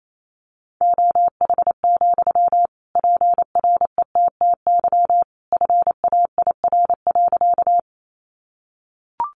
Morse code, heard as a keyed tone, about 28 wpm, 700 Hz. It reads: OH, PRETTY FAIR.
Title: Sentences from Top 1300 Words 28wpm